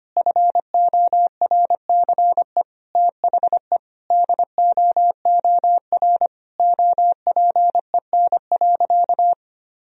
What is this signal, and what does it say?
Morse code, heard as a keyed tone, about 25 wpm, 700 Hz. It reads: FORCE THE DOOR OPEN.
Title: Sentences from Top 1000 Words - Review Part 3 - 25wpm